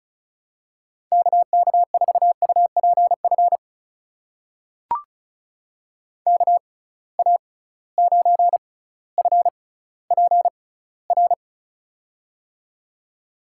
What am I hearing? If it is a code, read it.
Morse code, heard as a keyed tone, about 35 wpm, spaced as for 18 wpm, 700 Hz. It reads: KK4UPF KA9FPR
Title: USA Call Signs - 6 Characters - 6x Character Spacing - 35wpm